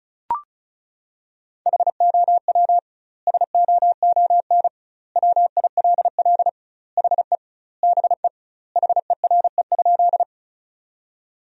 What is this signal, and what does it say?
Morse code, heard as a keyed tone, about 34 wpm, 700 Hz. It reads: HOW SOON WILL HE BE HERE?
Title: 2x Word-Spacing - Sentences from Top 400 Words - 35wpm